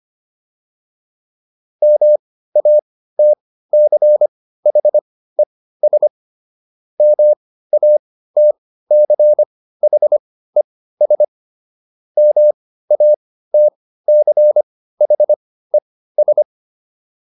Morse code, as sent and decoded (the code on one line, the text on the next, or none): -- .- - -.-. .... ... / -- .- - -.-. .... ... / -- .- - -.-. .... ...
MATCHES MATCHES MATCHES